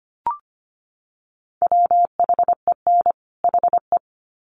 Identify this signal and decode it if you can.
Morse code, heard as a keyed tone, about 25 wpm, 700 Hz. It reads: WHEN HE
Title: Sets of 2 Words 25wpm